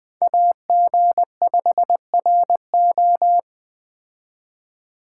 Morse code, heard as a keyed tone, about 21 wpm, 700 Hz. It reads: AG5RO
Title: Speed Racing - USA Call Signs 17wpm